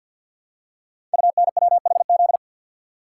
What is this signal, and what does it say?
Morse code, heard as a keyed tone, about 52 wpm, 700 Hz. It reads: UNWHZ
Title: Code Groups - 5 Letters - 50wpm